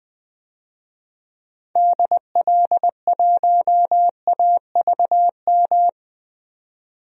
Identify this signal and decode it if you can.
Morse code, heard as a keyed tone, about 20 wpm, 700 Hz. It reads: DL1AVM
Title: DXCC-100 Call Signs - 6 Characters - 20wpm